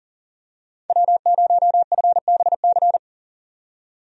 Morse code, heard as a keyed tone, about 40 wpm, 700 Hz. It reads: W0FBC
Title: USA Call Signs - 5 Characters - 40wpm